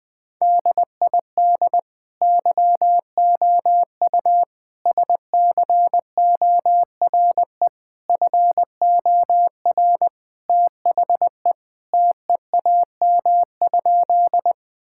Morse code, heard as a keyed tone, about 20 wpm, 700 Hz. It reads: DID YOU SCORE FOR THE TEAM?
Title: Sentences from Top 1600 Words 20wpm